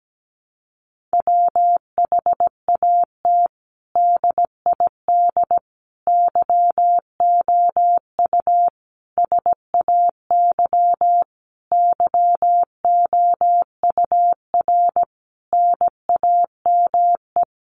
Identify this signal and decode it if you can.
Morse code, heard as a keyed tone, about 17 wpm, 700 Hz. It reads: WHAT DID YOU SAY YOUR NAME
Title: Sentences from Top 200 Words 17wpm